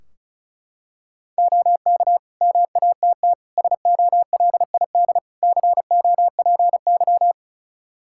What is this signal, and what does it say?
Morse code, heard as a keyed tone, about 36 wpm, 700 Hz. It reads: OK MATT SOLID COPY
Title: Practice QSO - Exchange 3 - More Info - 35wpm